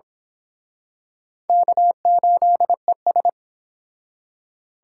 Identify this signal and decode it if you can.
Morse code, heard as a keyed tone, about 26 wpm, 700 Hz. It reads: K8ES
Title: Speed Racing - USA Call Signs 22wpm